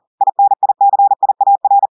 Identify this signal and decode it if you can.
Morse code, heard as a keyed tone, about 40 wpm, 800 Hz. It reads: INICIAR
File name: audios / portugais / portugais texte en clair 2 vitesse 40 mots minutes